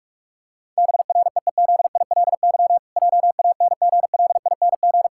Morse code, heard as a keyed tone, about 46 wpm, 700 Hz. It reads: BREEZILY JANGLING
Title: Pangram Challenge v2 - 45wpm